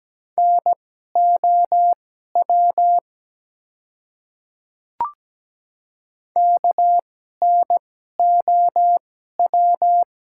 Morse code, heard as a keyed tone, about 17 wpm, spaced as for 12 wpm, 700 Hz. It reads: NOW KNOW
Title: 2x Character Spacing - Top 100 Words - 17wpm